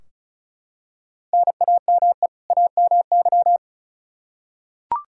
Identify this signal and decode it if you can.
Morse code, heard as a keyed tone, about 36 wpm, 700 Hz. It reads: NAME AMY
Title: Practice QSO - Exchange 1 - Beginning - 35wpm